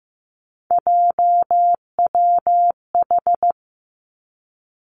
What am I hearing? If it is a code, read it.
Morse code, heard as a keyed tone, about 15 wpm, 700 Hz. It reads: JWH
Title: Three Challenging Letter-Number Combinations 15wpm